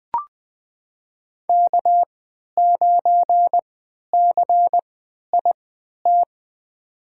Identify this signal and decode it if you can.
Morse code, heard as a keyed tone, about 20 wpm, spaced as for 11 wpm, 700 Hz. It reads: K9CIT
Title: USA Call Signs - 5 Characters - 3x Character Spacing - 20wpm